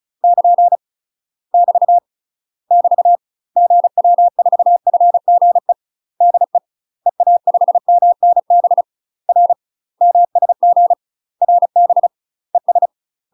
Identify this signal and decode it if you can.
Morse code, heard as a keyed tone, about 36 wpm, 700 Hz. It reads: <KN> <BT> <BT> GW4FGE DE EA5MNB R MSG RB ES
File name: UK MASTER QSOs TRAINING / 35WPM / UK MASTER QSOs 0036